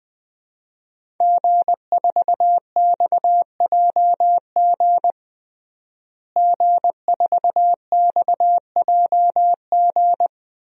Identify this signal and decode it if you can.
Morse code, heard as a keyed tone, about 21 wpm, 700 Hz. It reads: G4XJG G4XJG
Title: DXCC-100 Call Signs - 5 Characters - Repeated 3x - 20wpm